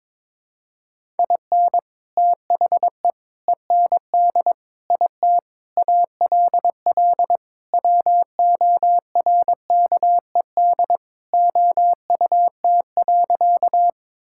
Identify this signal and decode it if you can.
Morse code, heard as a keyed tone, about 22 wpm, 700 Hz. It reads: IN THE END IT ALL WORKED OUT.
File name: practice / 22wpm / Sentences with 3-Word Phrases - 22wpm